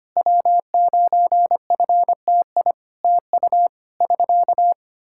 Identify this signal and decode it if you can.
Morse code, heard as a keyed tone, about 25 wpm, 700 Hz. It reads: W9FTI TU <SK>
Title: Practice QSO - Exchange 4 - Ending - 25wpm